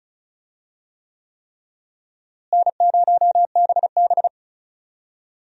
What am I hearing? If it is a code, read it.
Morse code, heard as a keyed tone, about 34 wpm, 700 Hz. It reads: N0BB
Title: USA Call Signs - 4 Characters - Repeated 3x - 35wpm